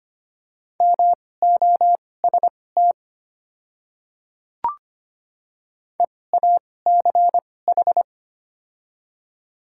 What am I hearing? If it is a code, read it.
Morse code, heard as a keyed tone, about 25 wpm, spaced as for 18 wpm, 690 Hz. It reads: MOST EACH
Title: 2x Character Spacing - Top 300 Words - 25wpm